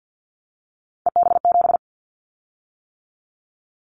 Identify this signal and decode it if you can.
Morse code, heard as a keyed tone, about 50 wpm, 700 Hz. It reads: EB7
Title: Callsign Prefixes for Top 25 DXCC - 50wpm